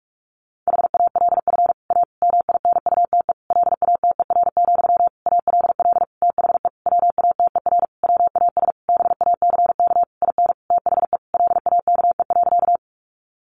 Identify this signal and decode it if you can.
Morse code, heard as a keyed tone, about 46 wpm, 700 Hz. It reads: HALF A MINUTE LATER, ALL THE WATER WAS BACK IN THE LAKE.
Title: Sentences from Top 1100 Words 45wpm